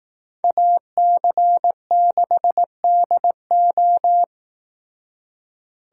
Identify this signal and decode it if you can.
Morse code, heard as a keyed tone, about 18 wpm, 700 Hz. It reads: AC6DO
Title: Speed Racing - USA 2x2 Call Signs 15wpm